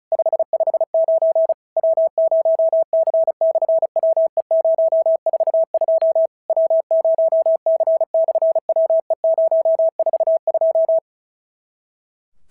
Morse code, heard as a keyed tone, about 36 wpm, 650 Hz. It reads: 559 W0C/WE042 W0C/WE042
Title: SOTA QSO Practice 35wpm